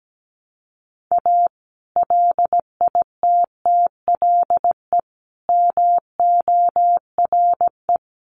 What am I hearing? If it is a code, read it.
Morse code, heard as a keyed tone, about 17 wpm, 700 Hz. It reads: A LITTLE MORE
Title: Sets of 5 Words Encore2 17wpm